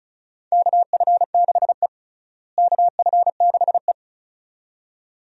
Morse code, heard as a keyed tone, about 34 wpm, 700 Hz. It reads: KF6E KF6E
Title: USA Call Signs - 4 Characters - Repeated 2x - 35wpm